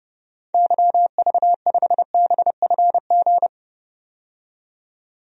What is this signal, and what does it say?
Morse code, heard as a keyed tone, about 30 wpm, 700 Hz. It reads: YV5BFG
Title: DXCC-100 Call Signs - 6 Characters - 30wpm